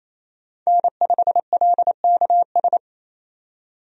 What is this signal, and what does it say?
Morse code, heard as a keyed tone, about 28 wpm, 700 Hz. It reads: N5LKS
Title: USA Call Signs - 5 Characters - 28wpm